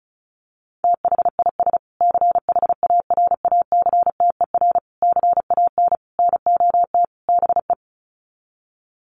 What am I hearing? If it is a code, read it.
Morse code, heard as a keyed tone, about 34 wpm, 700 Hz. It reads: THIS CHARACTER CAN NOT BE
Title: Sets of 5 Words Encore2 35wpm